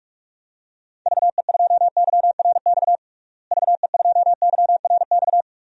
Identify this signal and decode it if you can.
Morse code, heard as a keyed tone, about 41 wpm, 700 Hz. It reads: VE2YRX VE2YRX
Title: DXCC-100 Call Signs - 6 Characters - Repeated 3x - 45wpm